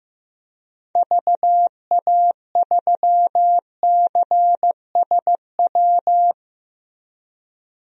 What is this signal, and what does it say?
Morse code, heard as a keyed tone, about 15 wpm, 700 Hz. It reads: VA3CSW
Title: DXCC-100 Call Signs - 6 Characters - Repeated 2x - 15wpm